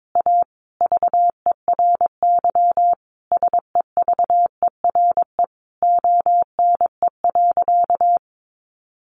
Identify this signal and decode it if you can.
Morse code, heard as a keyed tone, about 22 wpm, 700 Hz. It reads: A VERY SEVERE ONE.
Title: Sentences from Top 1400 Words 22wpm